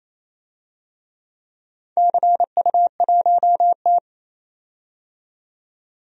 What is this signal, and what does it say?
Morse code, heard as a keyed tone, about 28 wpm, 700 Hz. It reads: CU1T
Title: DXCC-100 Call Signs - 4 Characters - 28wpm